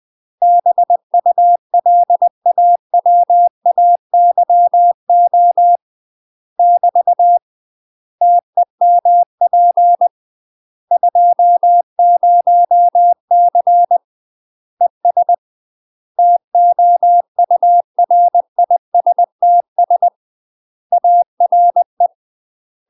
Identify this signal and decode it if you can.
Morse code, heard as a keyed tone, about 20 wpm, 700 Hz. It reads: BULAWAYO <BT> TEMP 20C ES TOURISTS ARE
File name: UK MASTER QSOs TRAINING / 20WPM / UK MASTER QSOs 0034